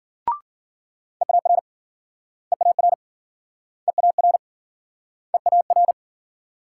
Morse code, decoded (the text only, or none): EAR EAR EAR EAR